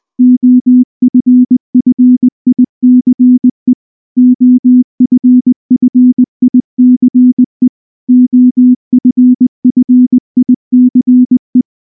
Morse code, heard as a keyed tone, about 19 wpm, 250 Hz. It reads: OFFICE OFFICE OFFICE
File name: morse / morse-training-20WPM-6